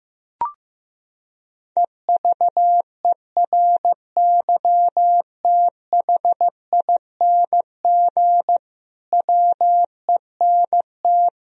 Morse code, heard as a keyed tone, about 16 wpm, 700 Hz. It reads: EVERYTHING WENT
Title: Sentences from Top 1000 Words - Review Part 4 - 15wpm